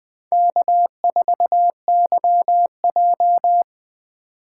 Morse code, heard as a keyed tone, about 20 wpm, 700 Hz. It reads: K4YJ
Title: USA Call Signs - 4 Characters - 3x Character Spacing - 20wpm